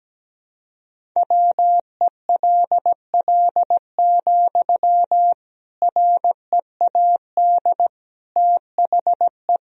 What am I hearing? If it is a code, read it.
Morse code, heard as a keyed tone, about 17 wpm, 700 Hz. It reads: WELL, READ THE
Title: Sentences from Top 300 Words and Abbreviations 17wpm